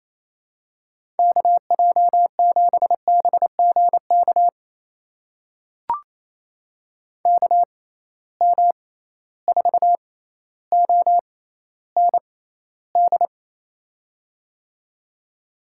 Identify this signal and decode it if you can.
Morse code, heard as a keyed tone, about 28 wpm, spaced as for 14 wpm, 700 Hz. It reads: KJ7BGK KM4OND
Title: USA Call Signs - 6 Characters - 6x Character Spacing - 28wpm